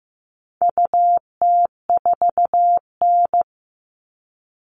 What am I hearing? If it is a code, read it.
Morse code, heard as a keyed tone, about 15 wpm, 700 Hz. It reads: UT4N
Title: All Cut Numbers - 4-Digit 15wpm